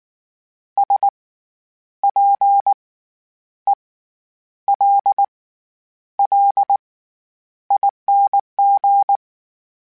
Morse code, heard as a keyed tone, about 19 wpm, spaced as for 9 wpm, 800 Hz. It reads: SPELLING
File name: Special INGs / Special INGs-mp3s / SlowToFaster ing @18